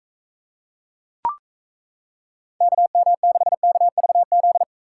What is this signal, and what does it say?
Morse code, heard as a keyed tone, about 35 wpm, 700 Hz. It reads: KM6KVZ
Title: Speed Racing - USA 2x3 Call Signs 28wpm